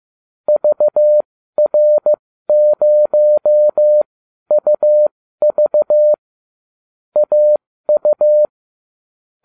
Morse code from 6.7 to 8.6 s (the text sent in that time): AU